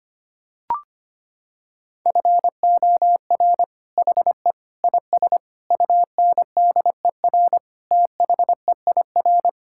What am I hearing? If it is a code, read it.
Morse code, heard as a keyed tone, about 25 wpm, 700 Hz. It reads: FOR HE IS UNDER THEIR
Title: Sentences from Top 1600 Words 25wpm